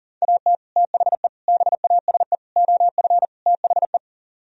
Morse code, heard as a keyed tone, about 51 wpm, 700 Hz. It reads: AT THE BASE OF THE